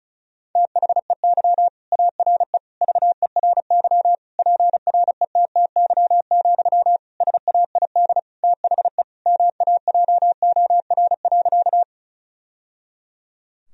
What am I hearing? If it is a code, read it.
Morse code, heard as a keyed tone, about 34 wpm, 700 Hz. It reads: THEY ARE VERY PRETTY, SAID THE MAJOR.
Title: Sentences from Top 1000 Words - Review Part 4 - 35wpm